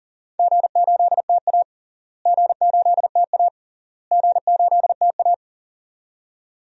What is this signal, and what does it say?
Morse code, heard as a keyed tone, about 40 wpm, 700 Hz. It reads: G8TA G8TA G8TA
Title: DXCC-100 Call Signs - 4 Characters - Repeated 3x - 40wpm